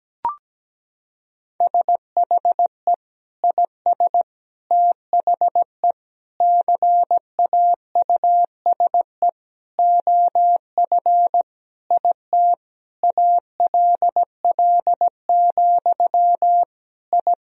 Morse code, heard as a keyed tone, about 17 wpm, 700 Hz. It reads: SHE IS THE CAUSE OF IT ALL, I
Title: Sentences from Top 1000 Words - Review Part 1 - 17wpm